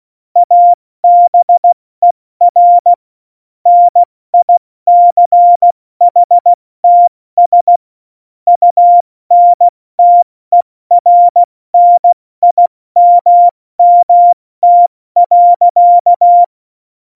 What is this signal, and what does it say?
Morse code, heard as a keyed tone, about 16 wpm, spaced as for 14 wpm, 700 Hz. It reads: ABER NICHTS UNTERNIMMT.
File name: Morsekurs / Witze / Witz 27. Oktober 2020, 70 F80